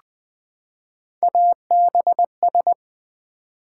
Morse code, heard as a keyed tone, about 20 wpm, 700 Hz. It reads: ABS
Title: Build A Word - 3 Letters At A Time - 6-Letter Words - 20wpm